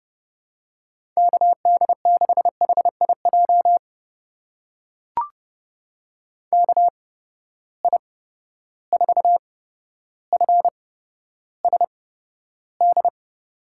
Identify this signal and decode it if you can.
Morse code, heard as a keyed tone, about 30 wpm, spaced as for 13 wpm, 700 Hz. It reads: KD6HIJ KI4FSD
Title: USA Call Signs - 8x Character Spacing - 30wpm